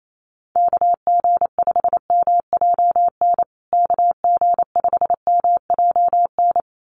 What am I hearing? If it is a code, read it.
Morse code, heard as a keyed tone, about 28 wpm, 700 Hz. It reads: KG5MJN KG5MJN
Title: USA Call Signs Repeated 2x 28wpm